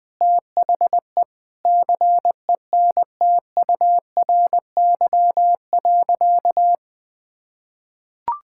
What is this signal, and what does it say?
Morse code, heard as a keyed tone, about 20 wpm, 700 Hz. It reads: THE CENTURY.
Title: Sentences with 5-Word Phrases - 20wpm